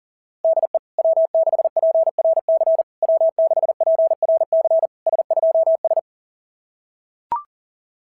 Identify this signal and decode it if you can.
Morse code, heard as a keyed tone, about 40 wpm, 660 Hz. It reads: DE W6PRC W6PRC S2S
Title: SOTA QSO Practice 40wpm